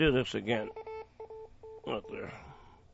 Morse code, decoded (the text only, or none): none